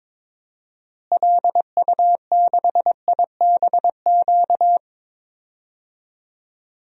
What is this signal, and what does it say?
Morse code, heard as a keyed tone, about 22 wpm, 700 Hz. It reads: LU6IBQ
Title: DXCC-100 Call Signs - 6 Characters - Repeated 2x - 22wpm